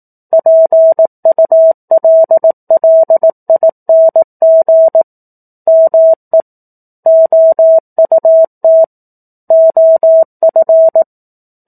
Morse code, heard as a keyed tone, about 18 wpm, 650 Hz. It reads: PULLING ME OUT OF
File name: 04-Chapter-04 18 WPM